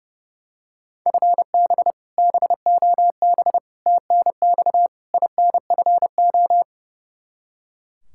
Morse code, heard as a keyed tone, about 31 wpm, 700 Hz. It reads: FB BOB TNX INFO
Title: K1USN SST QSO Practice 30wpm